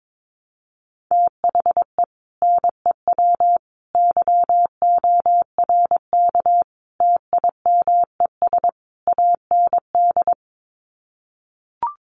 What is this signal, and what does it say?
Morse code, heard as a keyed tone, about 22 wpm, 700 Hz. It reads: THE NEW YORK TIMES AND
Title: Sets of 5 Words Encore 22wpm